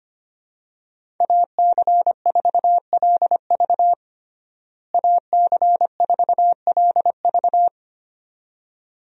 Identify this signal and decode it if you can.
Morse code, heard as a keyed tone, about 25 wpm, 700 Hz. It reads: AC4LV AC4LV
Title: USA Call Signs - 5 Characters - Repeated 2x - 25wpm